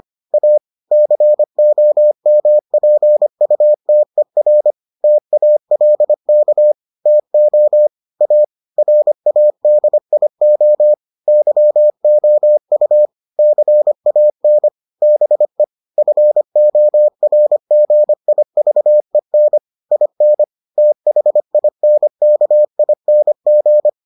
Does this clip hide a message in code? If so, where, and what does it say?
Yes: Morse code from 0.3 to 23.9 s, heard as a keyed tone, about 25 wpm, 600 Hz. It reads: A COMPUTER TALK TO A RADIO YOU CAN BE FORGIVEN IN THINKING